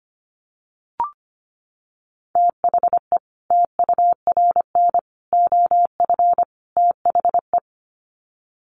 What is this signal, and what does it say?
Morse code, heard as a keyed tone, about 25 wpm, 700 Hz. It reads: THE TURN OF THE
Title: Sets of 4 Words 25wpm